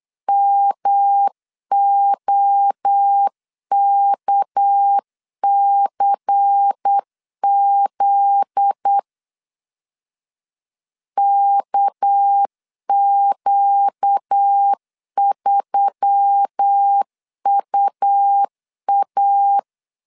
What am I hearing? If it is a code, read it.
Morse code, heard as a keyed tone, about 8 wpm, 800 Hz. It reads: MOKCZ KQ3UA
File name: LEZIONI / 8 YJ45 / 8 YJ 8wpm 3313